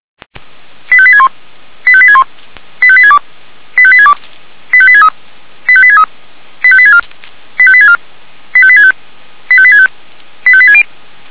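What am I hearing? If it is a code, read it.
Five-tone selective call (ZVEI), heard as quick runs of five tones, tones of about 70 ms. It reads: 85861 85861 85862 85862 85863 85863 85864 85864 85865 85865 85860
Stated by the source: ZVEI five-tone selective call received on SDR.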